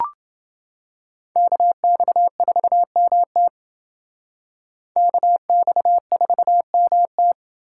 Morse code, heard as a keyed tone, about 28 wpm, 700 Hz. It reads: KX4MT KX4MT